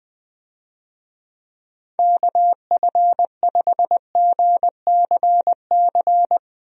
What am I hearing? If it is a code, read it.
Morse code, heard as a keyed tone, about 20 wpm, 700 Hz. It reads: KF5GCC
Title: USA Call Signs - 6 Characters - Repeated 2x - 20wpm